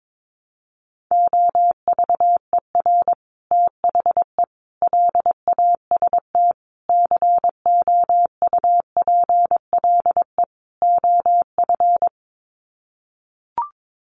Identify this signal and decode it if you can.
Morse code, heard as a keyed tone, about 22 wpm, 700 Hz. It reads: OVER THE LAST COUPLE OF
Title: Sets of 5 Words Encore 22wpm